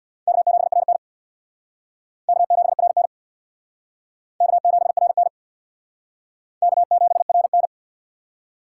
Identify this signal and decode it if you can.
Morse code, heard as a keyed tone, about 63 wpm, 700 Hz. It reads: K7RN K7RN K7RN K7RN